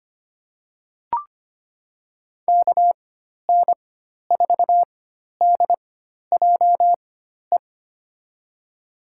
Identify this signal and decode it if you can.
Morse code, heard as a keyed tone, about 25 wpm, spaced as for 12 wpm, 700 Hz. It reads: KN4DJE
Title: USA Call Signs - 6 Characters - 4x Character Spacing - 25wpm